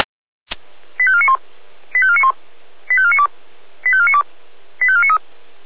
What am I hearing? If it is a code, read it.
Five-tone selective call (ZVEI), heard as quick runs of five tones, tones of about 70 ms. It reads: 85491 85491 85492 85492 85493